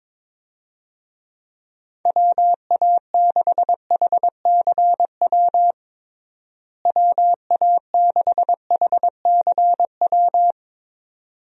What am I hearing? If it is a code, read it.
Morse code, heard as a keyed tone, about 22 wpm, 700 Hz. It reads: WA6HCW WA6HCW
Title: USA Call Signs - 6 Characters - Repeated 2x - 22wpm